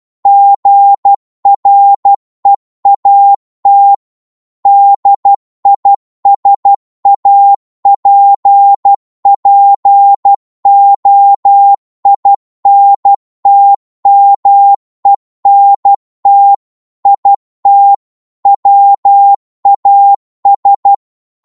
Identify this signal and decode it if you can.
Morse code, heard as a keyed tone, about 12 wpm, 800 Hz. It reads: GREAT DISAPPOINTMENT IT WAS
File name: cwmp3 / psg-1-12wpm